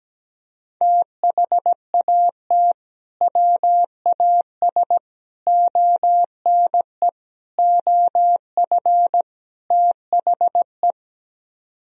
Morse code, heard as a keyed tone, about 17 wpm, 700 Hz. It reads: THAT WAS ONE OF THE
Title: Speed-Racing - Sets of 5 Words 17wpm